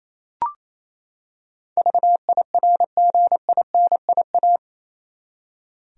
Morse code, heard as a keyed tone, about 28 wpm, 700 Hz. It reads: VIRGINIA
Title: USA Common Names - 28wpm